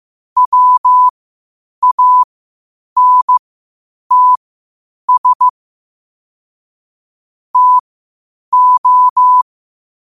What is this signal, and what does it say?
Morse code, heard as a keyed tone, about 15 wpm, spaced as for 8 wpm, 1000 Hz. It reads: WANTS TO
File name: qotdfiles / 8wpm